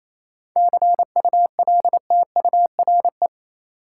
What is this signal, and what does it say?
Morse code, heard as a keyed tone, about 28 wpm, 700 Hz. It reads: CULTURE
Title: Sentences from Top 1000 Words - Review Part 6 - New Word Drill - 28wpm